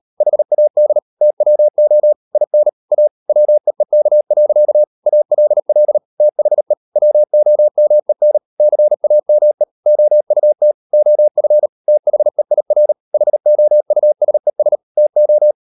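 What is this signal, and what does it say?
Morse code, heard as a keyed tone, about 38 wpm, 600 Hz. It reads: HAD TWO IN A WEEK. ALL THE WOMEN CAME OUT OF THEIR HOUSES TO
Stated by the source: Morse code practice at 35–39 wpm sending English text of Ken Follett's novel Never, from CW med Gnister.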